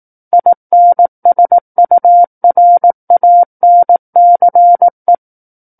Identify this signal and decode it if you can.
Morse code, heard as a keyed tone, about 18 wpm, 700 Hz. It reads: INSURANCE
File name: 18 WPM